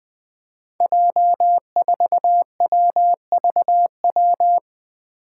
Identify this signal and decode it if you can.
Morse code, heard as a keyed tone, about 21 wpm, 700 Hz. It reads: J4WVW
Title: Five Challenging Letter-Number Combinations 20wpm